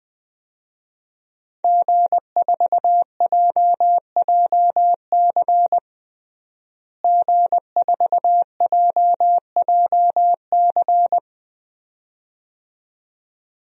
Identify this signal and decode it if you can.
Morse code, heard as a keyed tone, about 21 wpm, 700 Hz. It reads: G4JJC G4JJC
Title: DXCC-100 Call Signs - 5 Characters - Repeated 2x - 20wpm